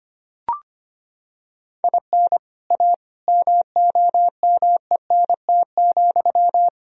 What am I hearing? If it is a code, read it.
Morse code, heard as a keyed tone, about 25 wpm, 700 Hz. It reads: IN A MOMENT,
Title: Sentences from Top 1000 Words - Review Part 2 - 25wpm